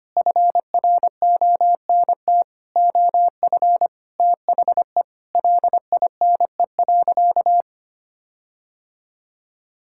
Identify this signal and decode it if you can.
Morse code, heard as a keyed tone, about 25 wpm, 700 Hz. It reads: FRONT OF THE LINE.
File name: practice / 25wpm / Sentences with 4-Word Phrases - 25wpm